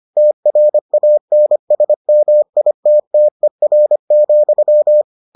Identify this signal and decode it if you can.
Morse code, heard as a keyed tone, about 25 wpm, 600 Hz. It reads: TRANSMITTER,